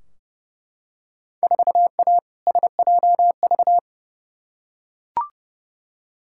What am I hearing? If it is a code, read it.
Morse code, heard as a keyed tone, about 30 wpm, 700 Hz. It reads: 4A SJV
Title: Field Day Exchanges Encore - 30wpm